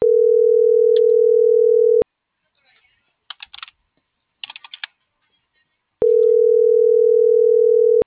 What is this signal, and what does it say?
Ringback tone (North America), 440 + 480 Hz.